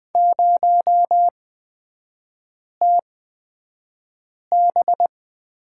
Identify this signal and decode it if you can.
Morse code, heard as a keyed tone, about 20 wpm, spaced as for 5 wpm, 700 Hz. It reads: 0TB
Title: 255 Lesson 41e - Period - Three Characters 20wpm